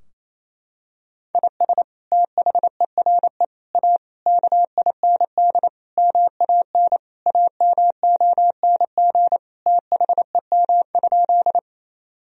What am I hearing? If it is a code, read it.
Morse code, heard as a keyed tone, about 28 wpm, 700 Hz. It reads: IS THERE A KIND MAN AMONG THEM?
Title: Sentences from Top 1000 Words - Review Part 3 - 28wpm